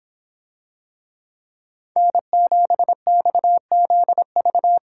Morse code, heard as a keyed tone, about 26 wpm, 700 Hz. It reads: N7XZV